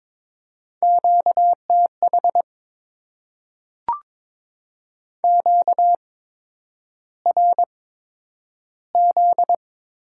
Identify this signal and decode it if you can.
Morse code, heard as a keyed tone, about 22 wpm, spaced as for 9 wpm, 700 Hz. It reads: QTH QRZ
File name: Top 90 QSO Elements - 8x Character Spacing - 22wpm